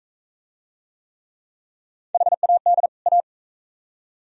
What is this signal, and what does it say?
Morse code, heard as a keyed tone, about 41 wpm, 700 Hz. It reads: HAD A